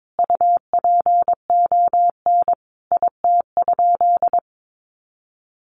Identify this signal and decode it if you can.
Morse code, heard as a keyed tone, about 22 wpm, 700 Hz. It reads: UPON IT?
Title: Sentences from Top 1500 Words 22wpm